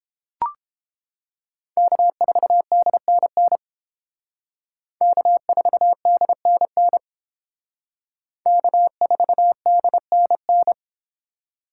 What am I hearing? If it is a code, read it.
Morse code, heard as a keyed tone, about 28 wpm, 700 Hz. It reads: K4DNN K4DNN K4DNN